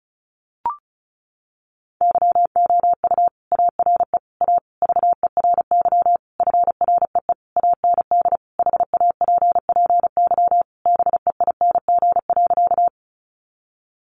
Morse code, heard as a keyed tone, about 34 wpm, 700 Hz. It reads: YOU ARE A VERY FREE AND HAPPY BEING.
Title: Sentences from Top 1000 Words 35wpm